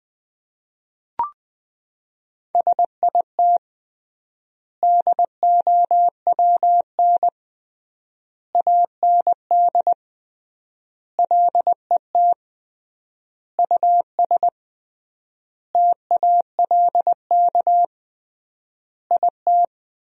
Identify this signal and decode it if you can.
Morse code, heard as a keyed tone, about 20 wpm, 700 Hz. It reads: SIT DOWN AND LET US TALK IT
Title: 3x Word-Spacing - Sentences from Top 500 Words - 20wpm